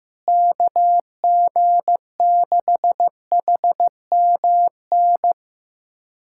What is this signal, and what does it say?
Morse code, heard as a keyed tone, about 15 wpm, 700 Hz. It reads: KG6HMN
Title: USA Callsign DE Callsign 15wpm